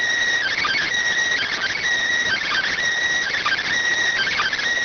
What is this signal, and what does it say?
Emergency Alert System (SAME) tones, heard as harsh buzzing bursts. It is heard from 0.4 to 4.6 s.